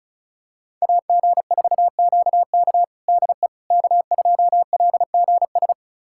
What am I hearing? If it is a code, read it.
Morse code, heard as a keyed tone, about 34 wpm, 700 Hz. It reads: AG4QK DE K2LGS
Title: USA Callsign DE Callsign - 5 Characters - 35wpm